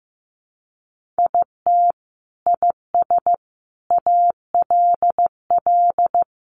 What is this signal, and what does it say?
Morse code, heard as a keyed tone, about 15 wpm, 700 Hz. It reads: IT IS ALL
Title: Sentences from Top 500 Words 15wpm